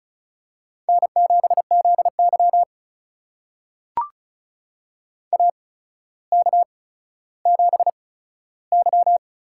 Morse code, heard as a keyed tone, about 36 wpm, spaced as for 15 wpm, 700 Hz. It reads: N7ZY AK7Y
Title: USA Call Signs - 4 Characters - 8x Character Spacing - 35wpm